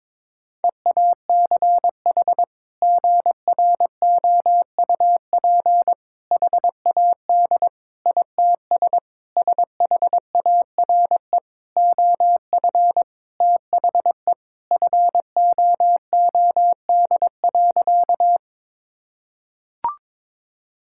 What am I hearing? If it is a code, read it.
Morse code, heard as a keyed tone, about 22 wpm, 700 Hz. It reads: EACH GROUP HAD ITS SHARE OF THE FOOD.
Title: Sentences from Top 1000 Words - Review Part 6 - 22wpm